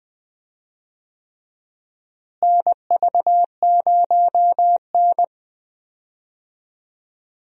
Morse code, heard as a keyed tone, about 20 wpm, 700 Hz. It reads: NV0N
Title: USA Call Signs - 4 Characters - Repeated 3x - 20wpm